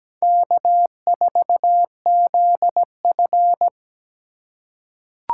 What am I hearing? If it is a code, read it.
Morse code, heard as a keyed tone, about 17 wpm, 700 Hz. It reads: K4ZF
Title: USA Call Signs - 4 Characters - 2x Character Spacing - 17wpm